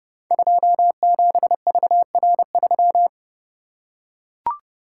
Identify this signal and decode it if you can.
Morse code, heard as a keyed tone, about 29 wpm, 700 Hz. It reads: 27VR3
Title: Code Groups - 5 Letter:Numbers - 30wpm